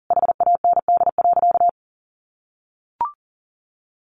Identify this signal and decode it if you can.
Morse code, heard as a keyed tone, about 40 wpm, 700 Hz. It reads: HAND.
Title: Sentences from Top 400 Words 40wpm